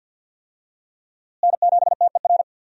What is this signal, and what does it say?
Morse code, heard as a keyed tone, about 50 wpm, 700 Hz. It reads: N7TER
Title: USA Call Signs - 5 Characters - Repeated 3x - 50wpm